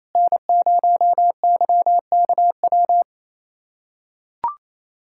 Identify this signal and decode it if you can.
Morse code, heard as a keyed tone, about 28 wpm, 700 Hz. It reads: N0YKW
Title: USA Callsign DE Callsign 28wpm